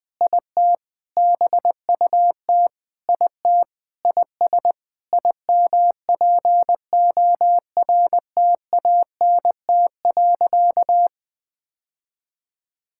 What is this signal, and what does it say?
Morse code, heard as a keyed tone, about 20 wpm, 700 Hz. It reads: IT BUT IT IS IMPORTANT.
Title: Sentences with 8-Word Phrases - 20wpm